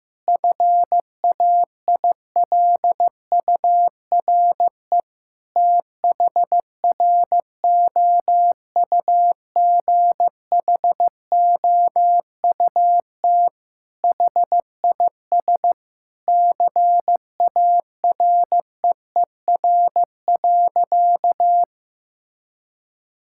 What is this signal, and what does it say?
Morse code, heard as a keyed tone, about 15 wpm, 700 Hz. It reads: FAILURE THROUGHOUT HIS CAREER.
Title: Sentences from Top 1000 Words - Review Part 6 - 15wpm